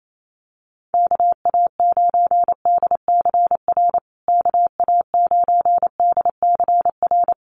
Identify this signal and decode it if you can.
Morse code, heard as a keyed tone, about 28 wpm, 700 Hz. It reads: KA9DCR KA9DCR
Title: USA Call Signs Repeated 2x 28wpm